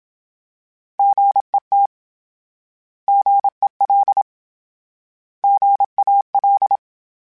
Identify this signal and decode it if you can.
Morse code, heard as a keyed tone, about 27 wpm, 800 Hz. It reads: GET GEL GAL
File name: Misc / Misc-mp3s / Misc @25